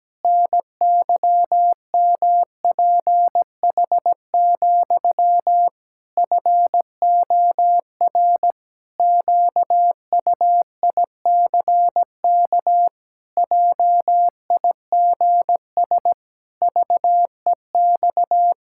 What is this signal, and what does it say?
Morse code, heard as a keyed tone, about 17 wpm, 700 Hz. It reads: NYMPH, FOR QUICK JIGS VEX
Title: Pangram Challenge v2 - 17wpm